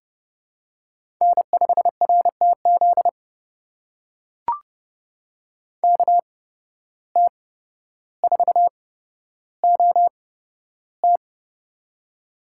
Morse code, heard as a keyed tone, about 30 wpm, spaced as for 13 wpm, 700 Hz. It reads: N5RTZ KT4OT
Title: USA Call Signs - 5 Characters - 8x Character Spacing - 30wpm